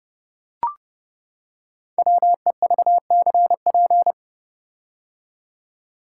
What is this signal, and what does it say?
Morse code, heard as a keyed tone, about 30 wpm, 700 Hz. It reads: WEVCP
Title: Code Groups - 5 Letter:Numbers - 30wpm